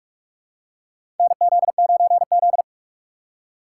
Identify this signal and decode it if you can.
Morse code, heard as a keyed tone, about 43 wpm, 700 Hz. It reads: NZ9Z
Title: USA Call Signs - 4 Characters - 8x Character Spacing - 45wpm